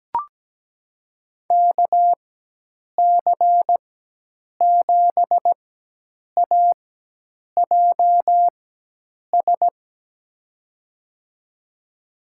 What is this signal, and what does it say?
Morse code, heard as a keyed tone, about 17 wpm, spaced as for 8 wpm, 700 Hz. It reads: KC7AJS
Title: USA Call Signs - 6 Characters - 4x Character Spacing - 17wpm